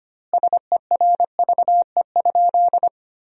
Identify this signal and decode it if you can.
Morse code, heard as a keyed tone, about 25 wpm, 700 Hz. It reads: SERVE?
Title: 2x Word-Spacing - Sentences from Top 400 Words - 25wpm